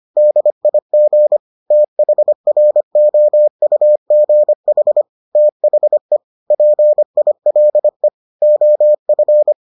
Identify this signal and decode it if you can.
Morse code, heard as a keyed tone, about 25 wpm, 600 Hz. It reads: DIG THROUGH THE PILE OF